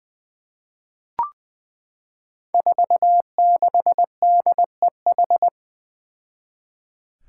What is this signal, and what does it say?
Morse code, heard as a keyed tone, about 20 wpm, 700 Hz. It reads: 46DEH
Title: Five Challenging Letter-Number Combinations 20wpm